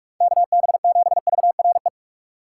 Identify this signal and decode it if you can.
Morse code, heard as a keyed tone, about 46 wpm, 700 Hz. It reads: KB7VRE
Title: USA Call Signs - 6 Characters - 6x Character Spacing - 45wpm